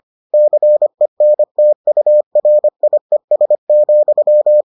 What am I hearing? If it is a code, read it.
Morse code, heard as a keyed tone, about 25 wpm, 600 Hz. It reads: CENTURIES,